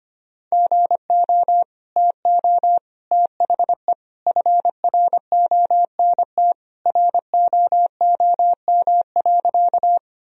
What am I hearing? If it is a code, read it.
Morse code, heard as a keyed tone, about 25 wpm, 700 Hz. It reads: GO TO THE FRONT ROOM.
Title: Sentences from Top 1000 Words - Review Part 2 - 25wpm